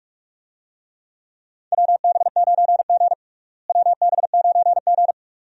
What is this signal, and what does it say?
Morse code, heard as a keyed tone, about 46 wpm, 700 Hz. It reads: WB9G WB9G
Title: USA Call Signs - 4 Characters - Repeated 2x - 45wpm